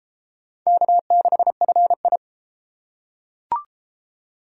Morse code, heard as a keyed tone, about 31 wpm, 700 Hz. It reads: K6FI